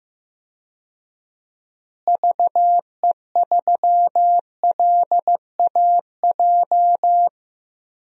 Morse code, heard as a keyed tone, about 15 wpm, 700 Hz. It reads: VE3LAJ
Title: DXCC-100 Call Signs - 6 Characters - Repeated 2x - 15wpm